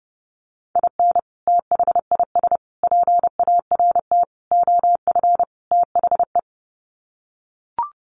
Morse code, heard as a keyed tone, about 30 wpm, 700 Hz. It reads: IN THIS PART OF THE
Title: Sets of 5 Words 30wpm